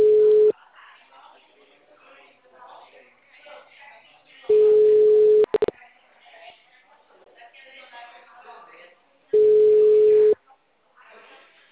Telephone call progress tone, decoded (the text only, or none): ringback tone